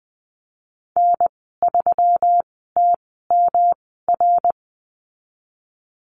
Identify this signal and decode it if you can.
Morse code, heard as a keyed tone, about 20 wpm, spaced as for 15 wpm, 700 Hz. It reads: N3TMR
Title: 161 Lesson 26g - Letter g - Call Signs 20wpm